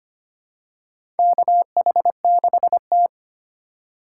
Morse code, heard as a keyed tone, about 25 wpm, 700 Hz. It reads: KH6T
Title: USA Call Signs - 4 Characters - 8x Character Spacing - 25wpm